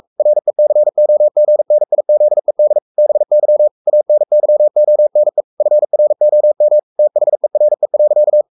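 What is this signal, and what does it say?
Morse code, heard as a keyed tone, about 44 wpm, 600 Hz. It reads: RECOGNIZED BY ANYONE FROM THERE.
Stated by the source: Morse code at 43 wpm sending Ken Follett's novel Never in English, from CW med Gnister.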